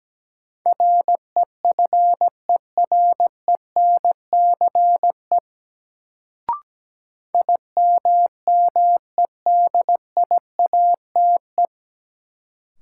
Morse code, heard as a keyed tone, about 17 wpm, 700 Hz. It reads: REFERENCE IMMEDIATE
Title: Sentences from Top 1200 Words - New Word Drill Alt - 17wpm